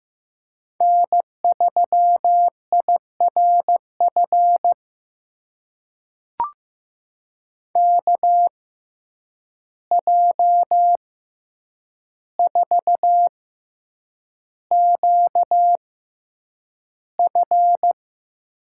Morse code, heard as a keyed tone, about 15 wpm, spaced as for 8 wpm, 700 Hz. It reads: N3IRF KJ4QF
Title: USA Call Signs - 6x Character Spacing - 15wpm